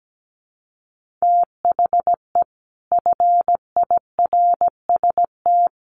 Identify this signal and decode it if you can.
Morse code, heard as a keyed tone, about 17 wpm, 700 Hz. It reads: THE FIRST
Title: Sets of 4 Words 17wpm